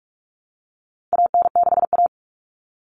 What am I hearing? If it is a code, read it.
Morse code, heard as a keyed tone, about 44 wpm, 700 Hz. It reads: AN6A
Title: Common Cut Numbers - 4-Digit 45wpm